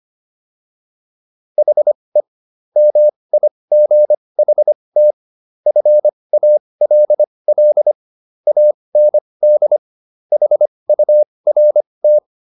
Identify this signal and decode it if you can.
Morse code, heard as a keyed tone, about 25 wpm, spaced as for 20 wpm, 600 Hz. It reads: HE MIGHT FALL AND HURT